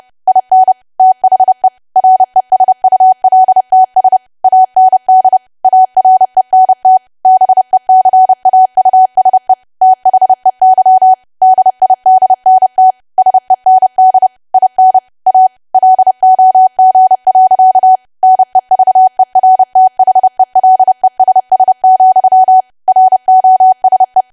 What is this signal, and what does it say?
Morse code, heard as a keyed tone, about 30 wpm, 750 Hz. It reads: IN THE RESULTS AND ARENT BECAUSE THEY DIDNT SEND IN A LOG. NEVERTHELESS, ROSE
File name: cwa / W1AW-Level3 / 30WPM